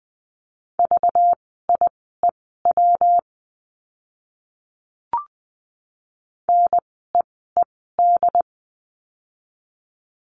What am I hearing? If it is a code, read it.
Morse code, heard as a keyed tone, about 20 wpm, spaced as for 15 wpm, 700 Hz. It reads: VIEW NEED
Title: 192 Lesson 31f - Letter v - Words 20wpm